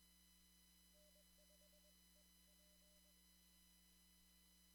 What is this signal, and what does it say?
Morse code, heard as a keyed tone, about 21 wpm, 600 Hz, with noise. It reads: N5EP
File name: CW / kt5j wpx cw-op-k5pi